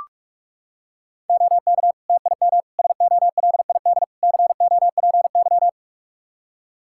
Morse code, heard as a keyed tone, about 41 wpm, 700 Hz. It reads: OK TIM SOLID COPY
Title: Practice QSO - Exchange 3 - More Info - 45wpm